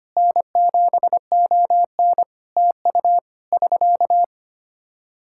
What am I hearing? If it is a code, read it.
Morse code, heard as a keyed tone, about 25 wpm, 700 Hz. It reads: N7ON TU <SK>
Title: Practice QSO - Exchange 4 - Ending - 25wpm